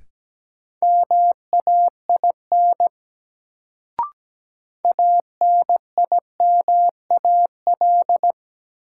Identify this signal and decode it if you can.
Morse code, heard as a keyed tone, about 17 wpm, 700 Hz. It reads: MAIN ANIMAL
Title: Sentences from Top 1300 Words - New Word Drill Alt - 17wpm